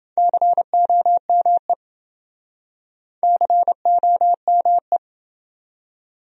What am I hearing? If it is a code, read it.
Morse code, heard as a keyed tone, about 28 wpm, 700 Hz. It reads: COME COME